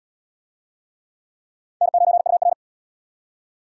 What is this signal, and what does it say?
Morse code, heard as a keyed tone, about 71 wpm, 700 Hz. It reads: N9UR